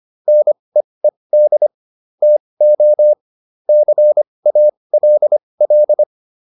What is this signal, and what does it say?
Morse code, heard as a keyed tone, about 25 wpm, spaced as for 20 wpm, 600 Hz. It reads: NEED TO CALL